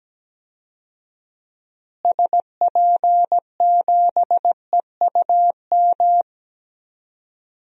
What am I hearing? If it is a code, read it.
Morse code, heard as a keyed tone, about 17 wpm, 700 Hz. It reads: SP7EUM
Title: DXCC-100 Call Signs - 6 Characters - Repeated 2x - 17wpm